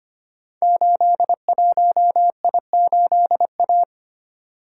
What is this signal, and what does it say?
Morse code, heard as a keyed tone, about 25 wpm, 700 Hz. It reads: 81I8A
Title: Code Groups - 5 Letter:Numbers - 25wpm